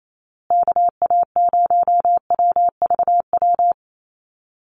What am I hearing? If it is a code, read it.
Morse code, heard as a keyed tone, about 28 wpm, 700 Hz. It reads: KA0WVW
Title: USA Call Signs Repeated 3x 28wpm